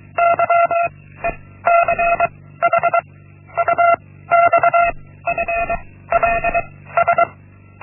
Morse code, heard as a keyed tone, about 23 wpm, 680 Hz. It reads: Y E C H U X F L S
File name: Linksmakalnis-1